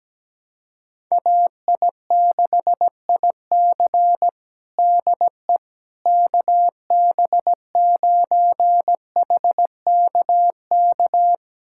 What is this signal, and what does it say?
Morse code, heard as a keyed tone, about 17 wpm, 700 Hz. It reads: AI6IC DE KB9HKK
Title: USA Callsign DE Callsign 17wpm